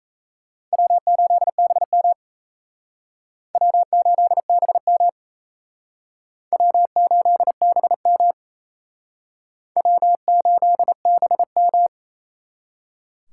Morse code, heard as a keyed tone, about 35 wpm, 700 Hz. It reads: W8BM W8BM W8BM W8BM
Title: Speed Racing - USA 1x2 Call Signs 28wpm